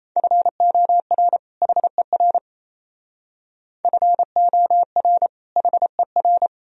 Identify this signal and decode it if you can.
Morse code, heard as a keyed tone, about 36 wpm, 700 Hz. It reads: FOR HER FOR HER